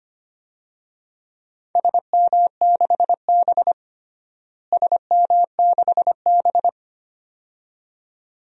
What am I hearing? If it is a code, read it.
Morse code, heard as a keyed tone, about 25 wpm, 700 Hz. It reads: SM6B SM6B
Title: DXCC-100 Call Signs - 4 Characters - Repeated 2x - 25wpm